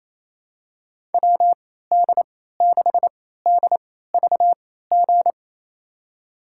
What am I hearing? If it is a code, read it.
Morse code, heard as a keyed tone, about 28 wpm, spaced as for 16 wpm, 700 Hz. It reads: WD6DVG
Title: USA Call Signs - 6 Characters - 3x Character Spacing - 28wpm